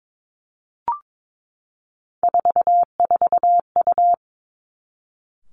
Morse code, heard as a keyed tone, about 22 wpm, 700 Hz. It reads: 44U
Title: Three Challenging Letter-Number Combinations 22wpm